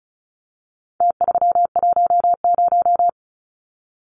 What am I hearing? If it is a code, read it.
Morse code, heard as a keyed tone, about 34 wpm, 700 Hz. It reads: T310
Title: Numbers - Contest Style - 4-Digit 35wpm